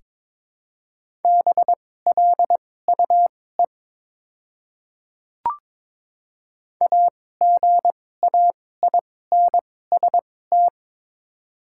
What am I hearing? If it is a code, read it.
Morse code, heard as a keyed tone, about 22 wpm, spaced as for 16 wpm, 700 Hz. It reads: BLUE AGAINST
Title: 2x Character Spacing - Top 500 Words - 22wpm